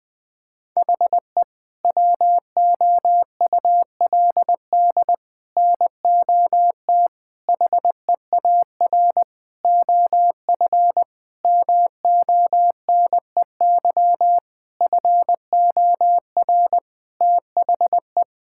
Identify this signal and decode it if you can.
Morse code, heard as a keyed tone, about 21 wpm, 700 Hz. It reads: HE WOULD NOT HEAR OF MONEY FOR THE
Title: Sentences from Top 1000 Words - Review Part 5 - 20wpm